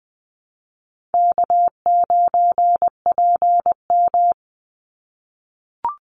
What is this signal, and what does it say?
Morse code, heard as a keyed tone, about 20 wpm, 700 Hz. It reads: K9PM
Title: NA SOTA Activator Callsigns 20wpm